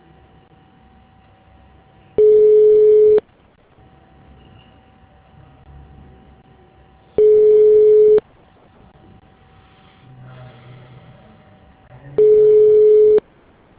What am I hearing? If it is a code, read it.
Ringback tone (Europe, 425 Hz).